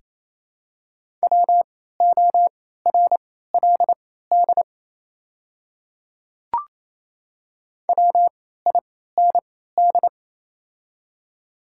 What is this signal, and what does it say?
Morse code, heard as a keyed tone, about 27 wpm, spaced as for 16 wpm, 700 Hz. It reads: WORLD WIND
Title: 3x Character Spacing - Top 500 Words - 28wpm